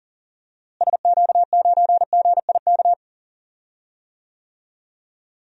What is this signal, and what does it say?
Morse code, heard as a keyed tone, about 40 wpm, 700 Hz. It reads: SQ9GIK
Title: DXCC-100 Call Signs - 6 Characters - Repeated 3x - 40wpm